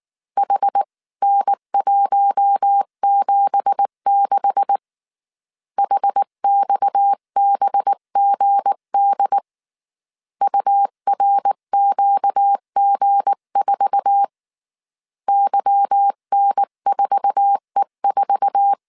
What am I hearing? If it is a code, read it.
Morse code, heard as a keyed tone, about 19 wpm, 800 Hz. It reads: HN176 HXBGD URQG4 YN4E4
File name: CW / DET / 1 9 1500C S C 18wpm 2539